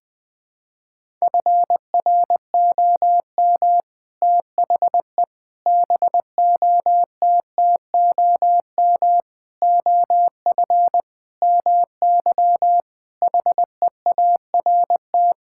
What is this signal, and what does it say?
Morse code, heard as a keyed tone, about 20 wpm, 700 Hz. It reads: FROM THE BOTTOM OF MY HEART
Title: Sets of 6 Words - 20wpm